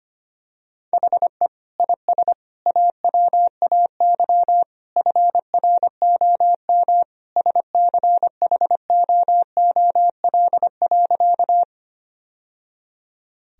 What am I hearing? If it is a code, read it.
Morse code, heard as a keyed tone, about 25 wpm, 700 Hz. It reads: HE IS AWAY FROM SCHOOL.
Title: Sentences from Top 1000 Words - Review Part 1 - 25wpm